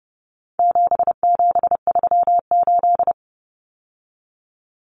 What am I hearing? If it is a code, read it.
Morse code, heard as a keyed tone, about 30 wpm, 700 Hz. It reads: 7738
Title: Numbers - 4-Digit 30wpm